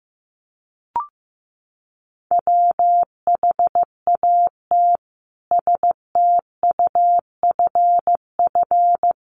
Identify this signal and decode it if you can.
Morse code, heard as a keyed tone, about 14 wpm, 700 Hz. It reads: WHAT STUFF
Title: Sentences from Top 1400 Words 15wpm